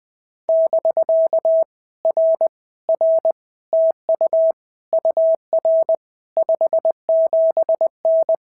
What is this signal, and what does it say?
Morse code, heard as a keyed tone, about 20 wpm, 650 Hz. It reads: <BK> R R TU UR 57N
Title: POTA QSO Practice - 20wpm